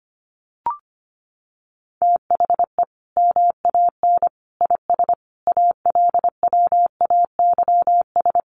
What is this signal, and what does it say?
Morse code, heard as a keyed tone, about 25 wpm, 700 Hz. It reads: THE MAN IS ALWAYS
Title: Sentences from Top 1400 Words 25wpm